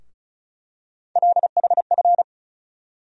Morse code, heard as a keyed tone, about 34 wpm, 700 Hz. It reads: LHF
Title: USA Call Sign Postfixes - 3 Letters 35wpm